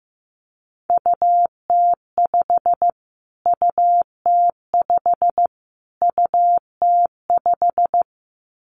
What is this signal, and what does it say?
Morse code, heard as a keyed tone, about 15 wpm, 700 Hz. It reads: UT5 UT5 UT5
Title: Callsign Prefixes for Top 25 DXCC - Repeated 3x - 15wpm